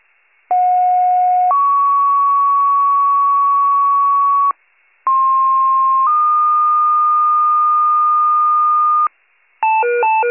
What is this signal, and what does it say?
Two-tone page, heard as a one-second tone, then a three-second tone, about 730 then 1120 Hz, about 1060 then 1220 Hz. It is heard from 0.5 to 9.1 s.
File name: pager / 98 RESCUE / wcalert